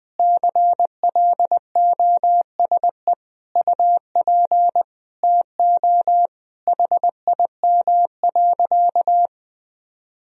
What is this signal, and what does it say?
Morse code, heard as a keyed tone, about 20 wpm, 700 Hz. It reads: CLOSE UP TO HIM.
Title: Sentences from Top 1000 Words - Review Part 1 - 20wpm